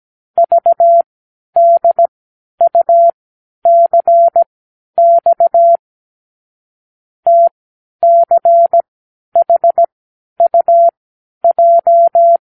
Morse code, heard as a keyed tone, about 17 wpm, spaced as for 10 wpm, 680 Hz. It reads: VDUCX TCHUJ
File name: k7qo / Lesson 36 (A-X)